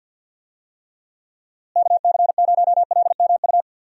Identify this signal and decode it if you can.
Morse code, heard as a keyed tone, about 50 wpm, 700 Hz. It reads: KC0LMU
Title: USA Call Signs - 6 Characters - Repeated 3x - 50wpm